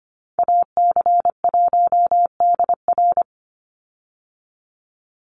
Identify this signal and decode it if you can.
Morse code, heard as a keyed tone, about 25 wpm, 700 Hz. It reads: AC1DR
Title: USA Call Signs 25wpm